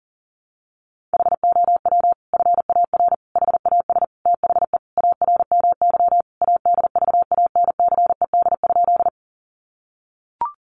Morse code, heard as a keyed tone, about 40 wpm, 700 Hz. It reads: HOW FAR HAS THE ARMY ADVANCED?
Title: Sentences from Top 1200 Words 40wpm